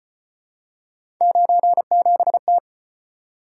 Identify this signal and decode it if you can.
Morse code, heard as a keyed tone, about 33 wpm, 700 Hz. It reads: 97T